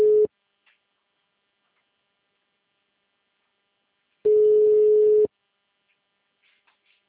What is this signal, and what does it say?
Ringback tone (Europe, 425 Hz).
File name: out- -2006- - - .2